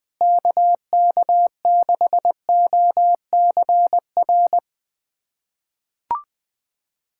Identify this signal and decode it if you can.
Morse code, heard as a keyed tone, about 20 wpm, 700 Hz. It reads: KK6OCR
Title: USA Callsign DE Callsign 20wpm